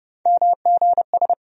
Morse code, heard as a keyed tone, about 30 wpm, 700 Hz. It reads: MGS
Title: USA Call Sign Postfixes - 3 Letters 30wpm